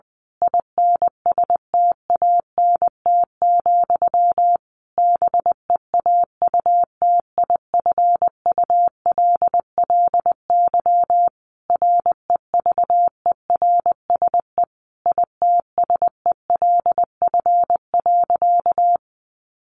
Morse code, heard as a keyed tone, about 20 wpm, 700 Hz. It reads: INSTANT, BEAUTIFULLY REVERSE ITSELF.
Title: A Sound Of Thunder 20wpm